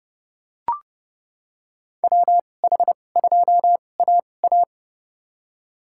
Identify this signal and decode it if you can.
Morse code, heard as a keyed tone, about 31 wpm, spaced as for 22 wpm, 700 Hz. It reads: WH2AA
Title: USA Call Signs - 5 Characters - 2x Character Spacing - 30wpm